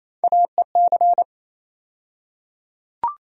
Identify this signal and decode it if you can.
Morse code, heard as a keyed tone, about 28 wpm, 700 Hz. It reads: AEC
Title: USA Call Sign Postfixes - 3 Letters 28wpm